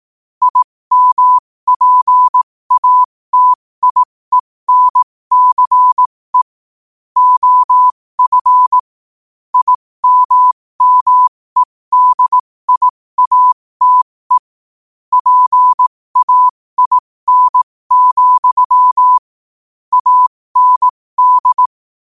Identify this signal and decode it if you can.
Morse code, heard as a keyed tone, about 18 wpm, spaced as for 15 wpm, 1000 Hz. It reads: IMPATIENCE OF IMMEDIATE PAIN, AND
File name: qotdfiles / 15wpm